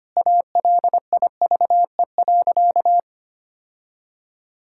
Morse code, heard as a keyed tone, about 25 wpm, 690 Hz. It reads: ALIVE.
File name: practice / 25wpm / Sentences with 2-Word Phrases - 25wpm